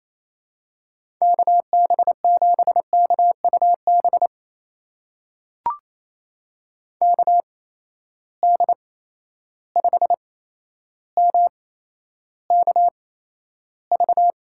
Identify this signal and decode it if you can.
Morse code, heard as a keyed tone, about 28 wpm, spaced as for 12 wpm, 700 Hz. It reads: KB7KUB KD5MKV
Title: USA Call Signs - 6 Characters - 8x Character Spacing - 28wpm